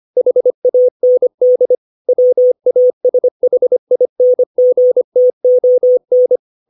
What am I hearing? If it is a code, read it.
Morse code, heard as a keyed tone, about 25 wpm, 500 Hz. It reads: HAND WASHINGTON
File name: 632 words / 632 Words 25WPM